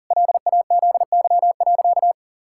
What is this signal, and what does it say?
Morse code, heard as a keyed tone, about 37 wpm, 700 Hz. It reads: LAZY.
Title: Pangram Challenge v2 - 40wpm